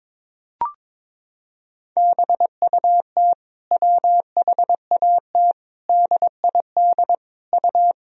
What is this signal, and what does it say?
Morse code, heard as a keyed tone, about 22 wpm, 700 Hz. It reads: BUT WHAT DID U
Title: Sentences from Top 200 Words and Abbreviations 22wpm